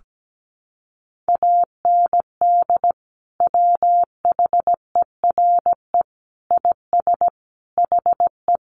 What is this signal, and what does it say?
Morse code, heard as a keyed tone, about 17 wpm, 700 Hz. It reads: AND WHERE IS HE
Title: Sentences from Top 200 Words 17wpm